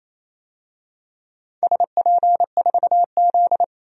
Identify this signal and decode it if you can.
Morse code, heard as a keyed tone, about 29 wpm, 690 Hz. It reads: SP4Z
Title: DXCC-100 Call Signs - 4 Characters - 28wpm